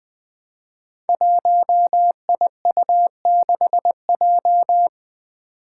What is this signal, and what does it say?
Morse code, heard as a keyed tone, about 20 wpm, 700 Hz. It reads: 1IU6J
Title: Five Challenging Letter-Number Combinations 20wpm